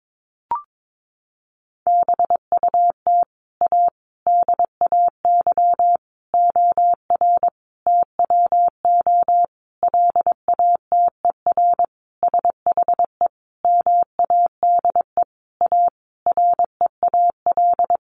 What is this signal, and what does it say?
Morse code, heard as a keyed tone, about 22 wpm, 700 Hz. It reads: BUT A DAY OR TWO LATER SHE MADE A REAL
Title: Sentences from Top 1100 Words 22wpm